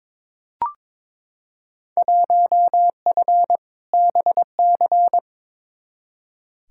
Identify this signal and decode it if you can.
Morse code, heard as a keyed tone, about 22 wpm, 700 Hz. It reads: 1F BC
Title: Field Day Exchanges Encore - 22wpm